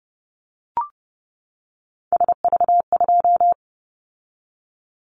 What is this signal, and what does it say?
Morse code, heard as a keyed tone, about 30 wpm, 700 Hz. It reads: SV2
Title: Callsign Prefixes for Top 50 DXCC - 30wpm